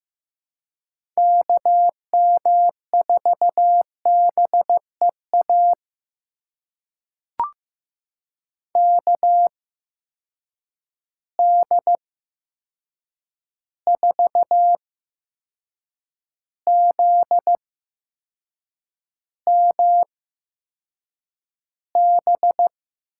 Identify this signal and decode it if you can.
Morse code, heard as a keyed tone, about 15 wpm, spaced as for 6 wpm, 700 Hz. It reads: KM4BEA KD4ZMB
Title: USA Call Signs - 6 Characters - 8x Character Spacing - 15wpm